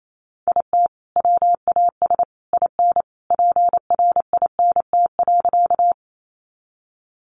Morse code, heard as a keyed tone, about 28 wpm, 700 Hz. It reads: IT WAS IN PRINT.
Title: Sentences from Top 1400 Words 28wpm